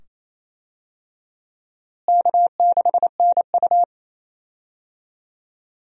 Morse code, heard as a keyed tone, about 27 wpm, 700 Hz. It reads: K6NU
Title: USA Call Signs - 4 Characters - Repeated 2x - 28wpm